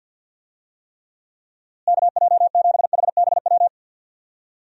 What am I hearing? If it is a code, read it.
Morse code, heard as a keyed tone, about 51 wpm, 700 Hz. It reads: KJ7HBW
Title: USA Call Signs - 6 Characters - Repeated 3x - 50wpm